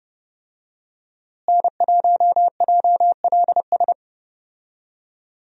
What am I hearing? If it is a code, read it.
Morse code, heard as a keyed tone, about 30 wpm, 700 Hz. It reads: N1JLS
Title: USA Call Signs - 5 Characters - Repeated 2x - 30wpm